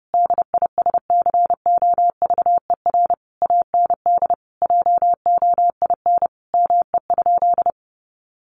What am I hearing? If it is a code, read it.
Morse code, heard as a keyed tone, about 30 wpm, 700 Hz. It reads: DISCOVER AND JOIN ME?
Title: Sentences from Top 1400 Words 30wpm